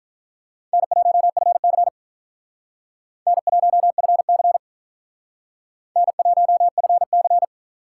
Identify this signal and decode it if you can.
Morse code, heard as a keyed tone, about 44 wpm, 700 Hz. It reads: N1FC N1FC N1FC